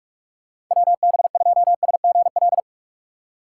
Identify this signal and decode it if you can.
Morse code, heard as a keyed tone, about 46 wpm, 690 Hz. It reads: WB2SGL
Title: USA Call Signs - 6x Character Spacing - 45wpm